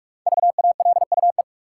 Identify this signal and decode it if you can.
Morse code, heard as a keyed tone, about 46 wpm, 700 Hz. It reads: VALUE
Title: Five Letter Words - 45wpm